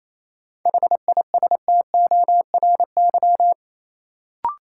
Morse code, heard as a keyed tone, about 28 wpm, 700 Hz. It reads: HISTORY
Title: Sentences from Top 600 Words - New Word Drill - 28wpm